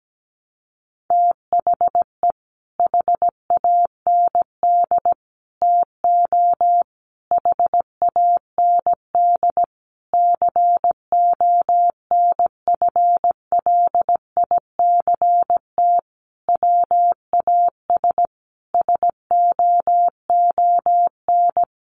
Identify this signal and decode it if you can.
Morse code, heard as a keyed tone, about 17 wpm, 700 Hz. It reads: THE HAND TO HAND CONFLICT WAS SOON
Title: Sentences from Top 1300 Words 17wpm